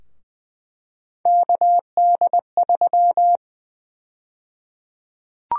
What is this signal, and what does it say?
Morse code, heard as a keyed tone, about 20 wpm, 700 Hz. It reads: KD3
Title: USA Call Sign Prefixes - 20wpm